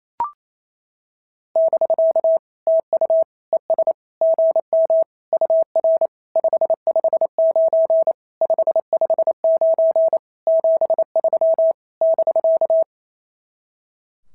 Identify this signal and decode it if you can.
Morse code, heard as a keyed tone, about 27 wpm, 650 Hz. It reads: <BK> TU ES GM UR 559 559 73 <BK>
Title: SOTA QSO Practice 28wpm